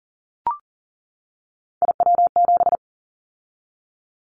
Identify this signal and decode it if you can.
Morse code, heard as a keyed tone, about 40 wpm, 700 Hz. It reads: IW7
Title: Callsign Prefixes for Top 25 DXCC - 40wpm